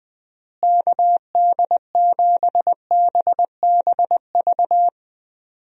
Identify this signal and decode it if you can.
Morse code, heard as a keyed tone, about 20 wpm, 700 Hz. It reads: KD7BBV
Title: USA Call Signs - 6 Characters - Repeated 2x - 20wpm